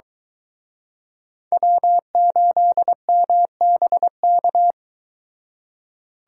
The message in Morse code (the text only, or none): W8MBK